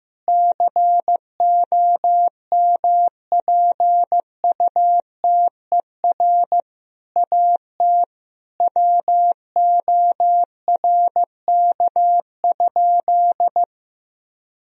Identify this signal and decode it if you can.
Morse code, heard as a keyed tone, about 15 wpm, 700 Hz. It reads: COMPUTER AT WORK?
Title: Sentences from Top 1000 Words - Review Part 3 - 15wpm